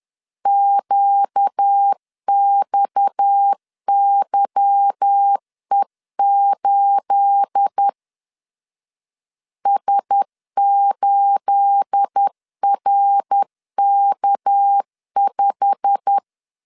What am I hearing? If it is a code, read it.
Morse code, heard as a keyed tone, about 10 wpm, 800 Hz. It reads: QXYE8 S8RK5
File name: CW / DET / 1 9 1500C S A 10wpm 4511